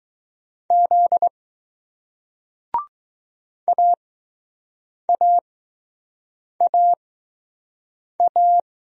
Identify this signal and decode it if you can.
Morse code, heard as a keyed tone, about 20 wpm, spaced as for 6 wpm, 700 Hz. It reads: Z AAAA